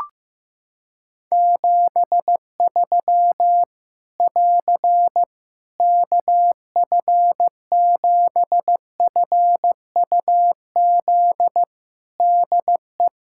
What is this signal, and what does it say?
Morse code, heard as a keyed tone, about 15 wpm, 700 Hz. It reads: 73 <AR> KF7FUZ DE
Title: Practice QSO - Exchange 4 - Ending - 15wpm